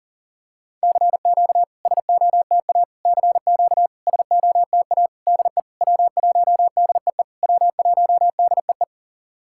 Morse code, heard as a keyed tone, about 40 wpm, 700 Hz. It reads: CQ SOTA CQ SOTA DE W1DEE W1DEE
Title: SOTA QSO Practice 40wpm